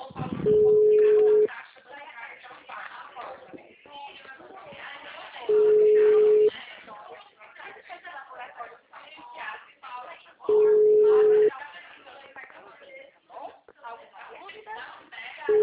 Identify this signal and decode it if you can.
Ringback tone (Europe, 425 Hz), with speech.